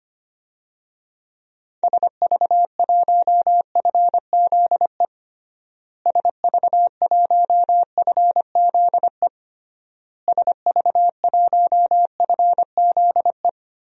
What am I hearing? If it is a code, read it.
Morse code, heard as a keyed tone, about 25 wpm, 700 Hz. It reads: SV1FZE SV1FZE SV1FZE
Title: DXCC-100 Call Signs - 6 Characters - Repeated 3x - 25wpm